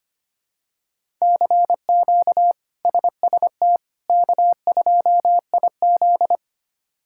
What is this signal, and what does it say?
Morse code, heard as a keyed tone, about 25 wpm, 700 Hz. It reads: CQ SST K2IZ
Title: K1USN SST QSO Practice - 25wpm